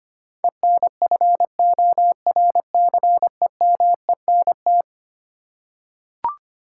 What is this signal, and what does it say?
Morse code, heard as a keyed tone, about 25 wpm, 700 Hz. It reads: ENFORCEMENT
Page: morse-code-ninja-practice-25wpm